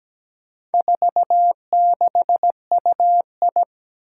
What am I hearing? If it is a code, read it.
Morse code, heard as a keyed tone, about 17 wpm, 700 Hz. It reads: 46UI
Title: Four Challenging Letter-Number Combinations 17wpm